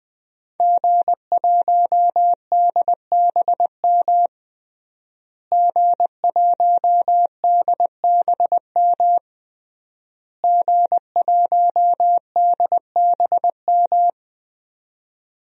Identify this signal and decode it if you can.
Morse code, heard as a keyed tone, about 20 wpm, 700 Hz. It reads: G1DBM G1DBM G1DBM
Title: DXCC-100 Call Signs - 5 Characters - Repeated 3x - 20wpm